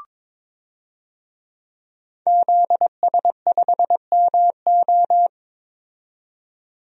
Morse code, heard as a keyed tone, about 22 wpm, 700 Hz. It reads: ZS5MO
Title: DXCC-100 Call Signs - 5 Characters - 22wpm